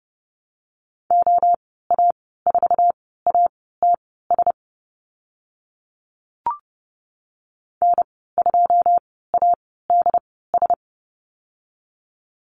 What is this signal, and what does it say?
Morse code, heard as a keyed tone, about 30 wpm, spaced as for 17 wpm, 700 Hz. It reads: OA4ATS N2ADS
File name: Lesson 15g - Number 5 - Call Signs 30wpm